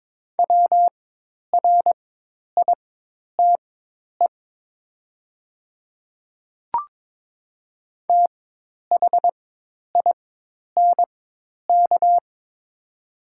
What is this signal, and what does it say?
Morse code, heard as a keyed tone, about 22 wpm, spaced as for 10 wpm, 700 Hz. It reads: WRITE THINK
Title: 4x Character Spacing - Top 500 Words - 22wpm